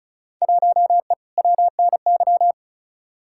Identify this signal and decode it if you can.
Morse code, heard as a keyed tone, about 36 wpm, 700 Hz. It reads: 1E WNY
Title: Field Day Exchanges - 35wpm